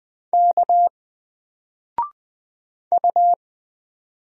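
Morse code, heard as a keyed tone, about 20 wpm, 700 Hz. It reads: K U
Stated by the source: Single letter and number Morse drill at 20 wpm, from N4LCD.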